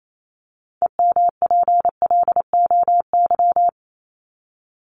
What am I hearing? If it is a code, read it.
Morse code, heard as a keyed tone, about 28 wpm, 700 Hz. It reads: EMPLOY
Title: Top 1751-2000 Words 28wpm